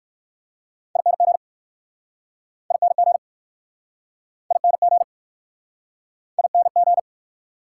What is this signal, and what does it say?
Morse code, heard as a keyed tone, about 60 wpm, 700 Hz. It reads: ING ING ING ING